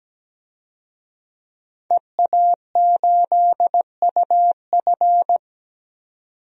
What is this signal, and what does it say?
Morse code, heard as a keyed tone, about 17 wpm, 700 Hz. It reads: EA8UF
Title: DXCC-100 Call Signs - 5 Characters - 17wpm